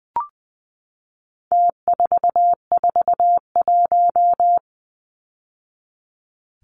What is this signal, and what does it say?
Morse code, heard as a keyed tone, about 20 wpm, 700 Hz. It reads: T441
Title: Numbers - Contest Style - 4-Digit 20wpm